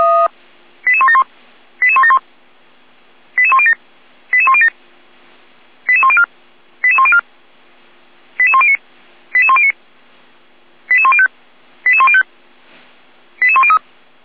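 Five-tone selective call (ZVEI), heard as quick runs of five tones, tones of about 70 ms. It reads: 88171 88171 88107 88107 88104 88104 88109 88109 88105 88105 88103
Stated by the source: ZVEI five-tone selective call 88387, received by SDR.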